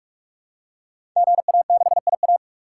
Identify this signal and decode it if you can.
Morse code, heard as a keyed tone, about 44 wpm, 700 Hz. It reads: GA6IA
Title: Code Groups - 5 Letter:Numbers - 45wpm